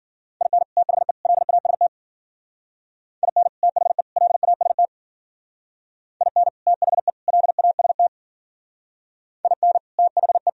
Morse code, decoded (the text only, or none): IN THE LAST IN THE LAST IN THE LAST IN THE